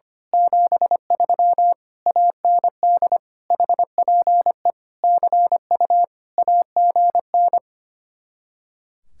Morse code, heard as a keyed tone, about 24 wpm, 700 Hz. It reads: 73 AND HPE CU AGN
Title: Speed-Racing Common QSO Phrases - 25wpm